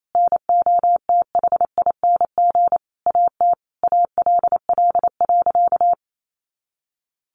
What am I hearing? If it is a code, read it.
Morse code, heard as a keyed tone, about 28 wpm, 700 Hz. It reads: NOTHING AT ALL.
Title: Sentences from Top 1300 Words 28wpm